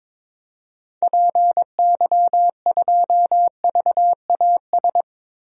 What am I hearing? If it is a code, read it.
Morse code, heard as a keyed tone, about 22 wpm, 700 Hz. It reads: PY2VAS
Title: DXCC-100 Call Signs - 6 Characters - Repeated 2x - 22wpm